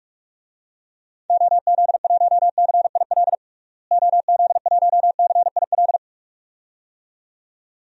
Morse code, heard as a keyed tone, about 48 wpm, 700 Hz. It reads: OZ1CIL OZ1CIL
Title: DXCC-100 Call Signs - 6 Characters - Repeated 2x - 45wpm